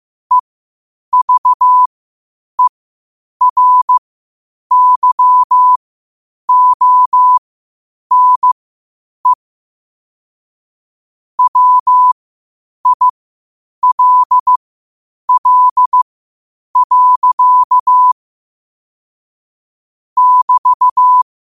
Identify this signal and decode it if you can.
Morse code, heard as a keyed tone, about 15 wpm, spaced as for 8 wpm, 1000 Hz. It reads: EVERYONE WILL. <BT>
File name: qotdfiles / 8wpm